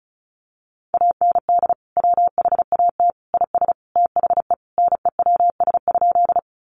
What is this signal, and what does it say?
Morse code, heard as a keyed tone, about 34 wpm, 700 Hz. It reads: AND WHAT IS THE NEWS?
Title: Sentences from Top 800 Words 35wpm